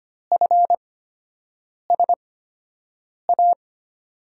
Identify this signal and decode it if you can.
Morse code, heard as a keyed tone, about 25 wpm, spaced as for 7 wpm, 700 Hz. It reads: FSA
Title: USA Call Signs - 6 Characters - 8x Character Spacing - 25wpm